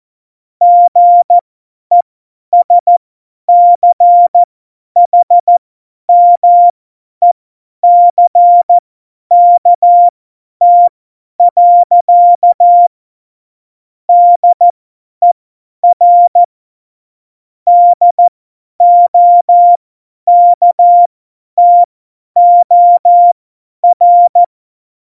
Morse code, heard as a keyed tone, about 14 wpm, spaced as for 10 wpm, 700 Hz. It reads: GESCHMECKT. DER DOKTOR
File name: Morsekurs / Witze / Witz 22. September 2020, 50 F70